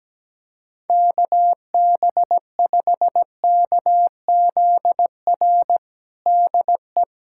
Morse code, heard as a keyed tone, about 17 wpm, 700 Hz. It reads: KB5KZR DE
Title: USA Callsign DE Callsign - 6 Characters - 17wpm